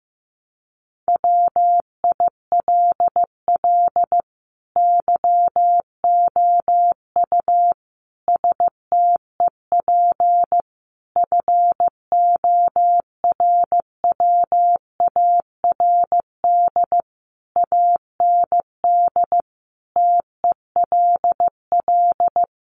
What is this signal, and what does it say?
Morse code, heard as a keyed tone, about 15 wpm, 700 Hz. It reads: WILL YOU STEP FORWARD AND TELL
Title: Sentences from Top 1300 Words 15wpm